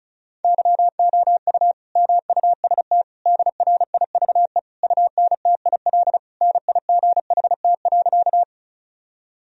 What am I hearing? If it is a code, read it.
Morse code, heard as a keyed tone, about 36 wpm, 700 Hz. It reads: YOU MUST DRIVE UNTIL NIGHT.
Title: Sentences from Top 1000 Words - Review Part 2 - 35wpm